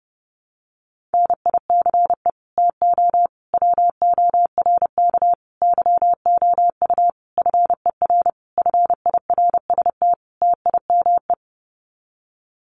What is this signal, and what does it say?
Morse code, heard as a keyed tone, about 28 wpm, 700 Hz. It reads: NICE TO WORK YOU FER FIRST TIME
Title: Common QSO Phrases 30wpm